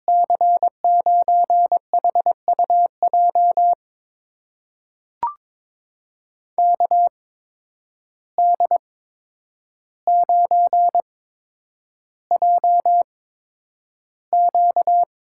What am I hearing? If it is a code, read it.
Morse code, heard as a keyed tone, about 22 wpm, spaced as for 9 wpm, 700 Hz. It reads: C9HUJ KD9JQ
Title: USA Call Signs - 6 Characters - 8x Character Spacing - 22wpm